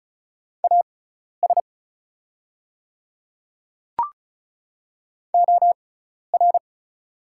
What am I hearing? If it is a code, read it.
Morse code, heard as a keyed tone, about 36 wpm, spaced as for 12 wpm, 690 Hz. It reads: AS OR
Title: 6x Character Spacing - Top 100 Words - 35wpm